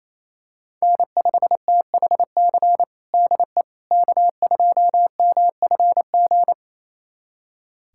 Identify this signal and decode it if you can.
Morse code, heard as a keyed tone, about 28 wpm, 700 Hz. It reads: N5THC DE K2MFG
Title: USA Callsign DE Callsign - 5 Characters - 28wpm